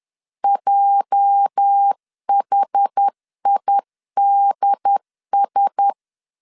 Morse code, heard as a keyed tone, about 10 wpm, 800 Hz. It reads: JHIDS
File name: LEZIONI / 9 286 10wpm 2735